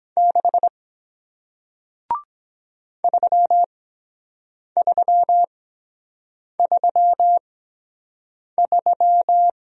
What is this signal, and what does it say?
Morse code, heard as a keyed tone, about 23 wpm, spaced as for 7 wpm, 700 Hz. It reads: 6 3333